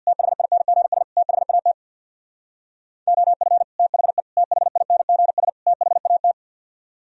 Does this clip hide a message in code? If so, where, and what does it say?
Yes: Morse code from 0.1 to 6.3 s, heard as a keyed tone, about 64 wpm, 690 Hz. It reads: THINGS THAT OF THE THINGS THAT